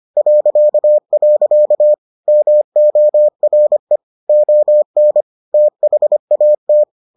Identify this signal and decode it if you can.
Morse code, heard as a keyed tone, about 25 wpm, 600 Hz. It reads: .. MORE ON THAT